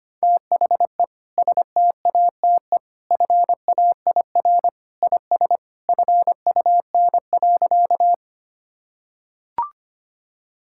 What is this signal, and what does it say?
Morse code, heard as a keyed tone, about 25 wpm, 700 Hz. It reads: THE STATE FAIR IS FUN.
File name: practice / 25wpm / Sentences with 2-Word Phrases - 25wpm